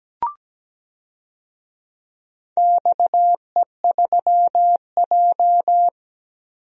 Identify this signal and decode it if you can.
Morse code, heard as a keyed tone, about 17 wpm, 700 Hz. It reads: XE3J
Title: DXCC-100 Call Signs - 4 Characters - Repeated 2x - 17wpm